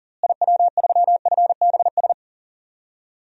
Morse code, heard as a keyed tone, about 37 wpm, 700 Hz. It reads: IW3FBS
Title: DXCC-100 Call Signs - 6 Characters - Repeated 2x - 40wpm